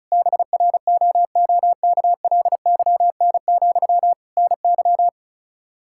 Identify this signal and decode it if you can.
Morse code, heard as a keyed tone, about 36 wpm, 700 Hz. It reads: BROOKLYN, NY
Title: Practice QSO - Exchange 1 - Beginning - 35wpm